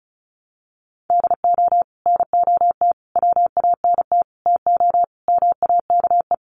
Morse code, heard as a keyed tone, about 34 wpm, 700 Hz. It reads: DO NOT WANT TO MAKE
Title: Sets of 5 Words Encore 35wpm